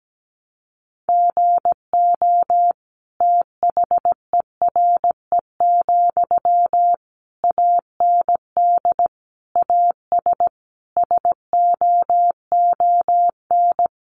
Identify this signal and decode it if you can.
Morse code, heard as a keyed tone, about 17 wpm, 700 Hz. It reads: GO THERE, AND AS SOON
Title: Sentences from Top 1300 Words 17wpm